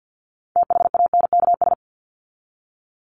Morse code, heard as a keyed tone, about 48 wpm, 700 Hz. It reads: THANKS
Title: Top 501-750 Words 50wpm